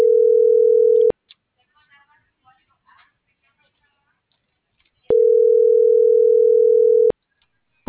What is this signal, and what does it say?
Ringback tone (North America), 440 + 480 Hz.
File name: CENTRAL MULTIMODAL / out alfredovelazquez